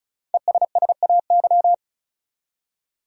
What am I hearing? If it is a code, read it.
Morse code, heard as a keyed tone, about 36 wpm, 700 Hz. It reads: ESSAY
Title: Five Letter Words - 35wpm